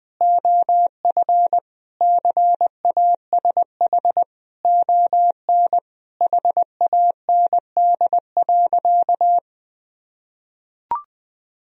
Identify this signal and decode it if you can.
Morse code, heard as a keyed tone, about 19 wpm, 700 Hz. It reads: OF CASH ON HAND.
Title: Sentences from Top 1000 Words - Review Part 4 - 20wpm